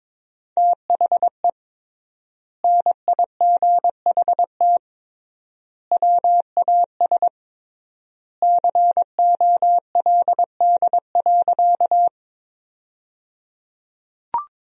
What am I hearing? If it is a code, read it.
Morse code, heard as a keyed tone, about 22 wpm, 700 Hz. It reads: THE NIGHT WAS COLD.
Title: 3x Word-Spacing - Sentences from Top 400 Words - 22wpm